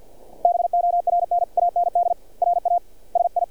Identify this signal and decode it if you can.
Morse code, heard as a keyed tone, about 46 wpm, spaced as for 40 wpm, 700 Hz. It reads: BORN AND RASI
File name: archived / CRSnet